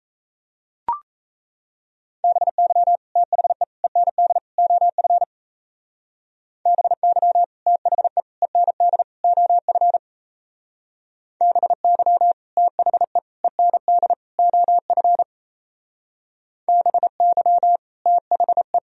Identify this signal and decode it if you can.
Morse code, heard as a keyed tone, about 35 wpm, 700 Hz. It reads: BY THE END OF BY THE END OF BY THE END OF BY THE